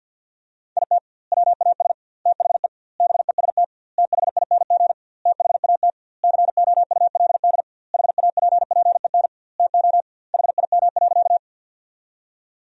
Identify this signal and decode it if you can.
Morse code, heard as a keyed tone, about 50 wpm, 700 Hz. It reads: IT WAS THE BEST THING THAT COULD HAPPEN TO HIM.
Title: 2x Word-Spacing - Sentences from Top 400 Words - 50wpm